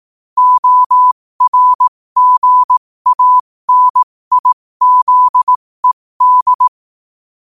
Morse code, heard as a keyed tone, about 18 wpm, spaced as for 15 wpm, 1000 Hz. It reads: ORGANIZED
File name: qotdfiles / 15wpm